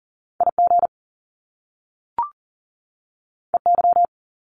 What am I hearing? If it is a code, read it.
Morse code, heard as a keyed tone, about 38 wpm, 700 Hz. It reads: IG EY
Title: Top Two Letter Combinations in Words 40wpm